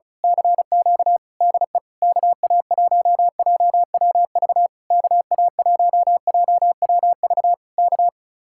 Morse code, heard as a keyed tone, about 33 wpm, 700 Hz. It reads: CQ DE KA1JWV KA1JWV K
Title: Speed-Racing Common QSO Phrases - 35wpm